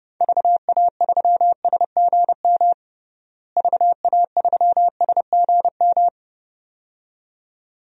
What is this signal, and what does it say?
Morse code, heard as a keyed tone, about 32 wpm, 700 Hz. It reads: VA3SGM VA3SGM
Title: DXCC-100 Call Signs - 6 Characters - Repeated 2x - 30wpm